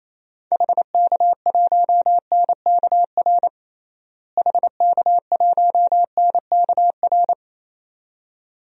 Morse code, heard as a keyed tone, about 28 wpm, 700 Hz. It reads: HK1NKR HK1NKR
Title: DXCC-100 Call Signs - 6 Characters - Repeated 2x - 28wpm